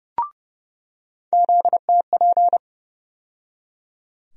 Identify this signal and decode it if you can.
Morse code, heard as a keyed tone, about 30 wpm, 700 Hz. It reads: ZTP
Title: USA Call Sign Postfixes - 3 Letters 30wpm